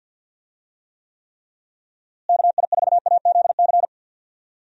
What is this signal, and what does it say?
Morse code, heard as a keyed tone, about 48 wpm, 700 Hz. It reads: KI4AZC
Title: USA Call Signs - 6 Characters - Repeated 3x - 50wpm